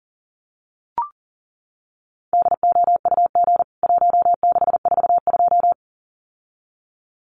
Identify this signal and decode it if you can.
Morse code, heard as a keyed tone, about 40 wpm, 700 Hz. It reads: DOUG 1642
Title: CWT Exchange 40wpm